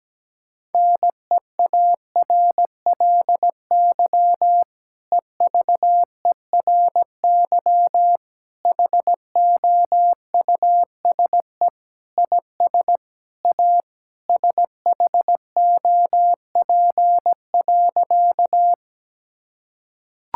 Morse code, heard as a keyed tone, about 17 wpm, 700 Hz. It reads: NEARLY EVERY HOUSE IS A SHOP.
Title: Sentences from Top 1600 Words 17wpm